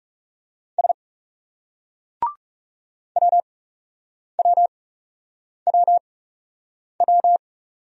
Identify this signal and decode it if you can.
Morse code, heard as a keyed tone, about 36 wpm, spaced as for 8 wpm, 700 Hz. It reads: S WWWW
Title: Speed-Racing Single Letters 30wpm